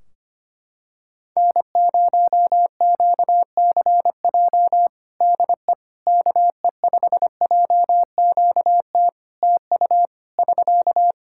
Morse code, heard as a keyed tone, about 25 wpm, 700 Hz. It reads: N0QCJ DE KE5JQT TU <SK>
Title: Practice QSO - Exchange 4 - Ending - 25wpm